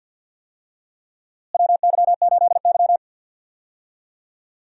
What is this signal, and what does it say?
Morse code, heard as a keyed tone, about 52 wpm, 700 Hz. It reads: WY8Y
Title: USA Call Signs - 4 Characters - Repeated 2x - 50wpm